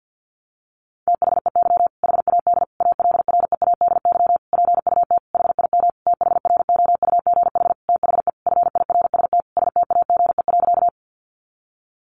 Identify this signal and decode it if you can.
Morse code, heard as a keyed tone, about 50 wpm, 700 Hz. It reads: THEY HAD ALREADY PUT HIM THROUGH THE FIRST STAGE.
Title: Sentences from Top 1200 Words 50wpm